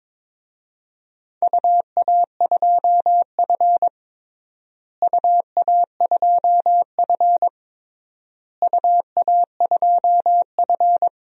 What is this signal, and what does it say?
Morse code, heard as a keyed tone, about 21 wpm, 700 Hz. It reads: UA2F UA2F UA2F
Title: DXCC-100 Call Signs - 4 Characters - Repeated 3x - 22wpm